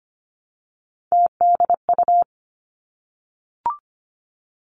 Morse code, heard as a keyed tone, about 25 wpm, 700 Hz. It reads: TDU
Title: Three Challenging Letter-Number Combinations 25wpm